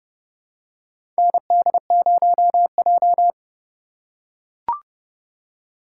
Morse code, heard as a keyed tone, about 30 wpm, 700 Hz. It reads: ND0J
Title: USA Call Signs - 4 Characters - 3x Character Spacing - 30wpm